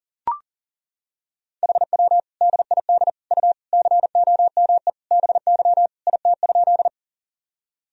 Morse code, heard as a keyed tone, about 38 wpm, 700 Hz. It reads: HW DID U COME BY IT?
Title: Sentences from Top 100 Words and Abbreviations 40wpm